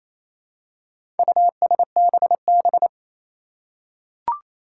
Morse code, heard as a keyed tone, about 28 wpm, 700 Hz. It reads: USBB
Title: Four Challenging Letter-Number Combinations 28wpm